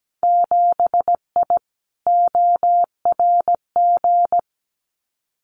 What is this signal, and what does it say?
Morse code, heard as a keyed tone, about 17 wpm, 700 Hz. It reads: 7I ORG
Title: Winter Field Day Exchanges Encore 17wpm